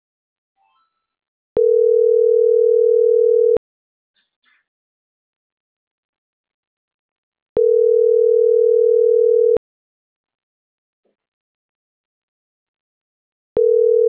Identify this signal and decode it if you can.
Ringback tone (North America), 440 + 480 Hz.